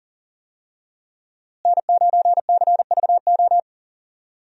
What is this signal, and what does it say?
Morse code, heard as a keyed tone, about 39 wpm, 700 Hz. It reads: N9CVO
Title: USA Call Signs - 5 Characters - Repeated 3x - 40wpm